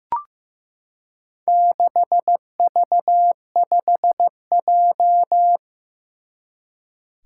Morse code, heard as a keyed tone, about 15 wpm, 700 Hz. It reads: 6V5J
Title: Four Challenging Letter-Number Combinations 15wpm